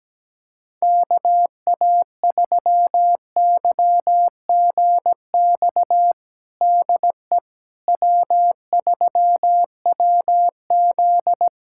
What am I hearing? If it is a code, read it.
Morse code, heard as a keyed tone, about 17 wpm, 700 Hz. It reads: KA3YGX DE W3WZ
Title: USA Callsign DE Callsign 17wpm